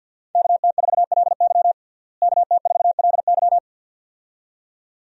Morse code, heard as a keyed tone, about 50 wpm, 700 Hz. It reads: KT4LY KT4LY
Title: USA Call Signs - 5 Characters - Repeated 2x - 50wpm